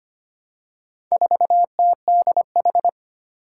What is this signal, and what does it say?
Morse code, heard as a keyed tone, about 25 wpm, 700 Hz. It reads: 4TDH
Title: Four Challenging Letter-Number Combinations 25wpm